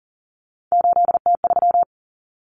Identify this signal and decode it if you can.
Morse code, heard as a keyed tone, about 40 wpm, 700 Hz. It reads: 8T3
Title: Common Cut Numbers - 3-Digit 40wpm